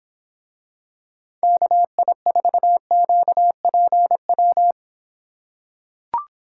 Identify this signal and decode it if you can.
Morse code, heard as a keyed tone, about 26 wpm, 700 Hz. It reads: KI4QPW